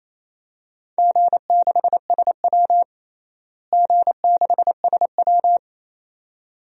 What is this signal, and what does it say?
Morse code, heard as a keyed tone, about 28 wpm, 700 Hz. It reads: G6SW G6SW
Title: DXCC-100 Call Signs - 4 Characters - Repeated 2x - 28wpm